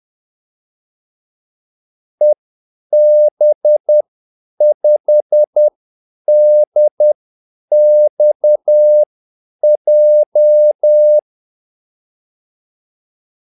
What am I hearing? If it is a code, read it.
Morse code, heard as a keyed tone, about 10 wpm, spaced as for 8 wpm, 600 Hz. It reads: EB5DXJ
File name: Callsigns-10-8wpm